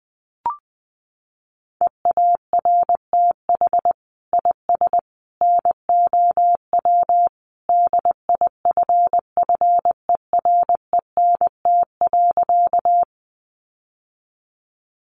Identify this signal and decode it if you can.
Morse code, heard as a keyed tone, about 20 wpm, 700 Hz. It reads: EARTH IS NOW DIFFERENT.
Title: Sentences from Top 600 Words 20wpm